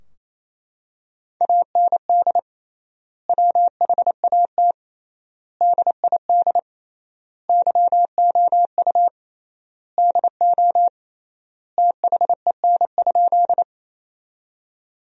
Morse code, heard as a keyed tone, about 28 wpm, 700 Hz. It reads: AND WHAT DID YOU DO THEN?
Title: 3x Word-Spacing - Sentences from Top 100 Words - 28wpm